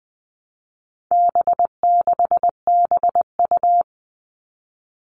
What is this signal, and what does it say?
Morse code, heard as a keyed tone, about 20 wpm, 700 Hz. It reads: B6BU
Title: All Cut Numbers - 4-Digit 20wpm